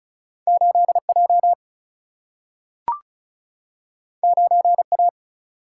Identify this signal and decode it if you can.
Morse code, heard as a keyed tone, about 36 wpm, 700 Hz. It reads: 8 J 9 A
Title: Single-Letter Single-Number 35wpm